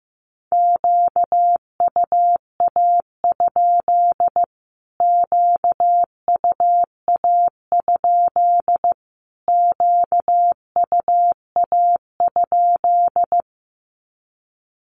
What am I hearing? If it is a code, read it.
Morse code, heard as a keyed tone, about 15 wpm, 700 Hz. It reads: QUA? QUA? QUA?
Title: Q-Codes Repeated 3x 15wpm